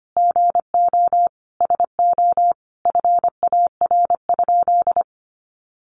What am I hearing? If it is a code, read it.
Morse code, heard as a keyed tone, about 25 wpm, 700 Hz. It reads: GO SO FAR?
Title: Sentences from Top 300 Words 25wpm